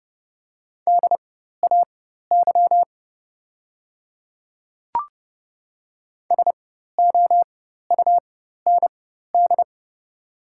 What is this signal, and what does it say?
Morse code, heard as a keyed tone, about 30 wpm, spaced as for 14 wpm, 690 Hz. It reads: DAY SOUND
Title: 4x Character Spacing - Top 100 Words - 30wpm